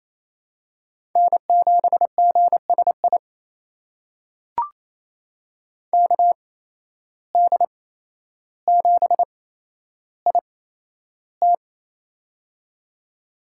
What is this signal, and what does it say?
Morse code, heard as a keyed tone, about 28 wpm, spaced as for 12 wpm, 700 Hz. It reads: N7GSI KD7IT
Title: USA Call Signs - 5 Characters - 8x Character Spacing - 28wpm